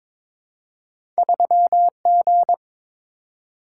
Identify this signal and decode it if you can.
Morse code, heard as a keyed tone, about 22 wpm, 700 Hz. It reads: 3G
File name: Two Letter-Number Combinations - 22wpm